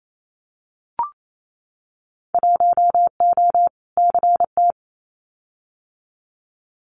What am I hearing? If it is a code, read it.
Morse code, heard as a keyed tone, about 28 wpm, 700 Hz. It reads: 1O CT
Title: Winter Field Day Exchanges 28wpm